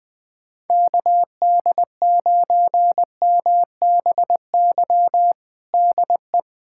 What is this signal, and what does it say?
Morse code, heard as a keyed tone, about 20 wpm, 700 Hz. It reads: KD9MBY DE
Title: USA Callsign DE Callsign 20wpm